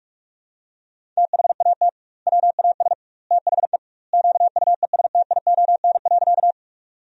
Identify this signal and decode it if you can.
Morse code, heard as a keyed tone, about 46 wpm, 700 Hz. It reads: THAT WAS THE QUESTION.
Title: 2x Word-Spacing - Sentences from Top 400 Words - 45wpm